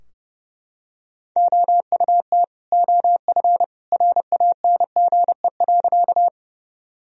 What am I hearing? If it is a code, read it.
Morse code, heard as a keyed tone, about 32 wpm, 700 Hz. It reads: OUT OF RANGE.
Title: Sentences from Top 1000 Words - Review Part 4 - 30wpm